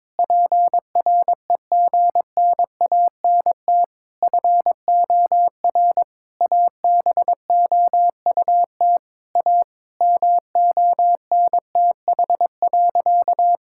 Morse code, heard as a keyed tone, about 21 wpm, 700 Hz. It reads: PREGNANT FOR ABOUT A MONTH.
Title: Sentences from Top 1000 Words - Review Part 6 - 22wpm